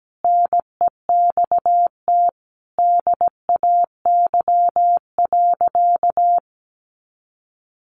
Morse code, heard as a keyed tone, about 17 wpm, 700 Hz. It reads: NEXT DAY.
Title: Sentences from Top 1400 Words 17wpm